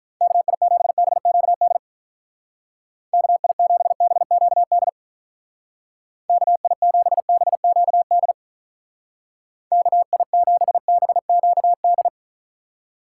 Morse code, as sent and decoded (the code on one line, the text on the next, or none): -.- .. --... -... --.- -.. / -.- .. --... -... --.- -.. / -.- .. --... -... --.- -.. / -.- .. --... -... --.- -..
KI7BQD KI7BQD KI7BQD KI7BQD